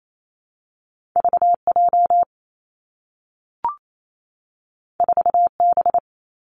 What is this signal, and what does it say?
Morse code, heard as a keyed tone, about 28 wpm, 700 Hz. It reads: V J 4 B
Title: Two Challenging Letter-Number Combinations 28wpm